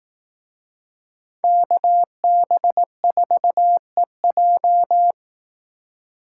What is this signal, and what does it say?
Morse code, heard as a keyed tone, about 18 wpm, 700 Hz. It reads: KB4EJ
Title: Speed Racing - USA 2x2 Call Signs 15wpm